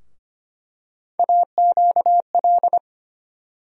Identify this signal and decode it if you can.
Morse code, heard as a keyed tone, about 25 wpm, 700 Hz. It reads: AQL
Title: USA Call Sign Postfixes - 3 Letters 25wpm